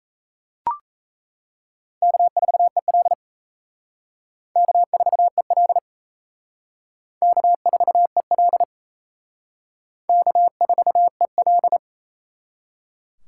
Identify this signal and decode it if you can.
Morse code, heard as a keyed tone, about 36 wpm, 700 Hz. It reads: K4EL K4EL K4EL K4EL